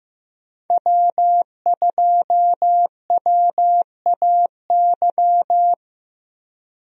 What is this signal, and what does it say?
Morse code, heard as a keyed tone, about 15 wpm, 700 Hz. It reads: W2WAY
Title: USA Call Signs - 5 Characters - 15wpm